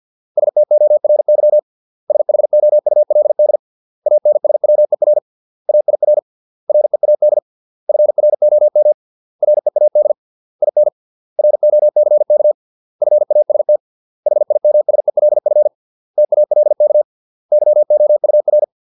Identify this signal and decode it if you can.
Morse code, heard as a keyed tone, about 50 wpm, 600 Hz. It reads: STORY SHOULD ANSWER AIR READ FROM READ IN ROCK FAST HIMSELF TALK YOUR